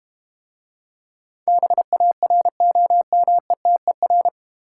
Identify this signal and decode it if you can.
Morse code, heard as a keyed tone, about 32 wpm, 700 Hz. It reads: BAROMETER